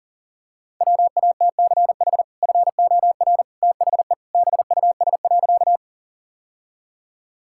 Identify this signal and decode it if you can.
Morse code, heard as a keyed tone, about 40 wpm, 700 Hz. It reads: WATCH FOR THE BUS.
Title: Sentences from Top 1000 Words - Review Part 6 - 40wpm